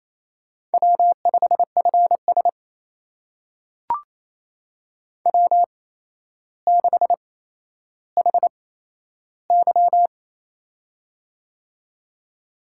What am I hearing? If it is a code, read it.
Morse code, heard as a keyed tone, about 27 wpm, spaced as for 12 wpm, 700 Hz. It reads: W5FS W6HY
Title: USA Call Signs - 4 Characters - 8x Character Spacing - 28wpm